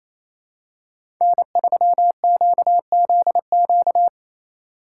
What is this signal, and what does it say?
Morse code, heard as a keyed tone, about 28 wpm, 690 Hz. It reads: N3QZQ
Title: USA Call Signs - 5 Characters - 4x Character Spacing - 28wpm